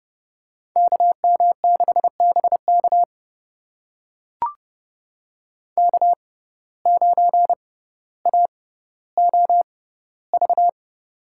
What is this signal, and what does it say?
Morse code, heard as a keyed tone, about 30 wpm, spaced as for 15 wpm, 700 Hz. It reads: KM6BK K9AOV
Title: USA Call Signs - 5 Characters - 6x Character Spacing - 30wpm